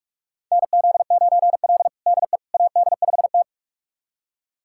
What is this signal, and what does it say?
Morse code, heard as a keyed tone, about 46 wpm, 700 Hz. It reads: NZ9L DE AD5T
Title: USA Callsign DE Callsign - 4 Characters - 45wpm